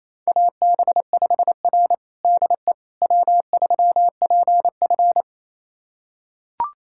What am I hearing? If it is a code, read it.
Morse code, heard as a keyed tone, about 27 wpm, 700 Hz. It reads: AB5R DE W3PF
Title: USA Callsign DE Callsign - 4 Characters - 28wpm